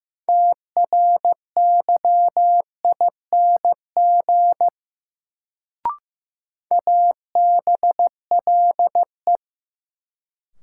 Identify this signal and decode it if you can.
Morse code, heard as a keyed tone, about 15 wpm, 700 Hz. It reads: TRYING ABLE
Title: Sets of 2 Words - New Word Drill - 15wpm